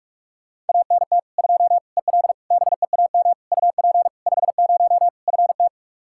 Morse code, heard as a keyed tone, about 46 wpm, 700 Hz. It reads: ANT 2 EL BEAM UP 50 FT
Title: Practice QSO - Exchange 2 - Info - 45wpm